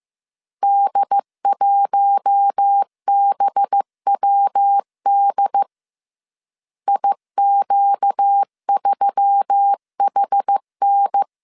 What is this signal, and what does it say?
Morse code, heard as a keyed tone, about 15 wpm, 800 Hz. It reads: D1BWD IQ3HN
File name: CW / DET / 1 6 1500C S C 14wpm 2951